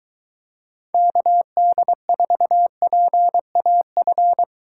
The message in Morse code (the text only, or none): KD4PAF